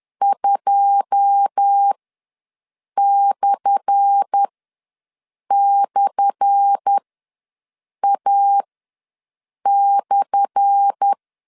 Morse code, heard as a keyed tone, about 10 wpm, spaced as for 6 wpm, 800 Hz. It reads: 2//A/
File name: LEZIONI / 10 SIGNS / 10A SLASH ws 10wpm 4121